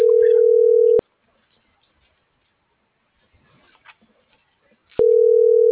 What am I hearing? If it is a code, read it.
Ringback tone (North America), 440 + 480 Hz.